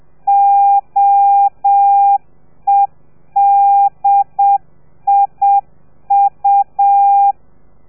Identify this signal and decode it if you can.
Morse code, heard as a keyed tone, about 7 wpm, 800 Hz. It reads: OEDIU